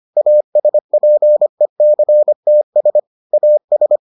Morse code, heard as a keyed tone, about 25 wpm, 600 Hz. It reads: ASPECTS AS